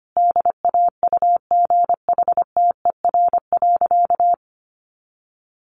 Morse code, heard as a keyed tone, about 25 wpm, 700 Hz. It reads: DAUGHTER.
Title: Sentences from Top 1500 Words 25wpm